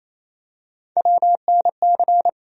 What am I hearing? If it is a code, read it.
Morse code, heard as a keyed tone, about 28 wpm, 700 Hz. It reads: WNC
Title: USA Call Sign Postfixes - 3 Letters 28wpm